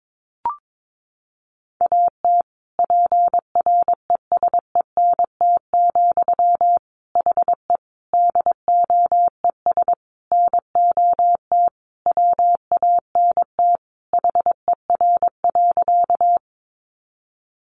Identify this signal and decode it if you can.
Morse code, heard as a keyed tone, about 22 wpm, 700 Hz. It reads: AT PRESENT, HE DOES NOT WANT HER.
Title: Sentences from Top 800 Words 22wpm